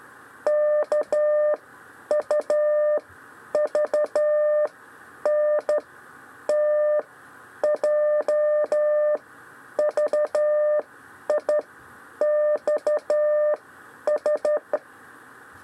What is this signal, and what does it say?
Morse code, heard as a keyed tone, about 13 wpm, spaced as for 9 wpm, 600 Hz. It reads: KUVNT JVIXS
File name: 6 lezione cw 1